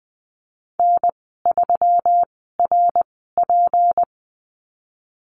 Morse code, heard as a keyed tone, about 19 wpm, spaced as for 14 wpm, 700 Hz. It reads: N3RP
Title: 154 Lesson 25g - Letter p - Call Signs 20wpm